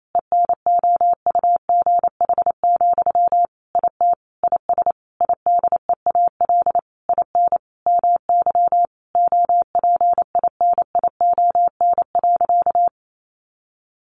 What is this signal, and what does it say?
Morse code, heard as a keyed tone, about 28 wpm, 700 Hz. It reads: ENOUGH, IT IS IDEAL IN MY OPINION.
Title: Sentences from Top 1400 Words 28wpm